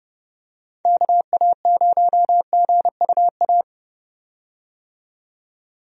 Morse code, heard as a keyed tone, about 30 wpm, 700 Hz. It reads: KA0GUA
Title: USA Call Signs - 6 Characters - 30wpm